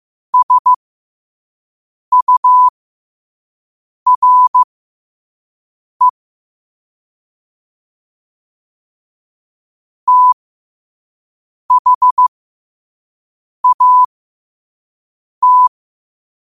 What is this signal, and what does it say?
Morse code, heard as a keyed tone, about 15 wpm, spaced as for 5 wpm, 1000 Hz. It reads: SURE THAT